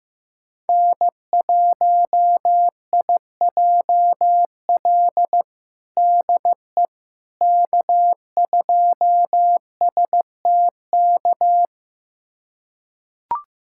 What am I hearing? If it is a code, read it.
Morse code, heard as a keyed tone, about 15 wpm, 700 Hz. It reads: N1IJL DE K2STK
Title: USA Callsign DE Callsign - 5 Characters - 15wpm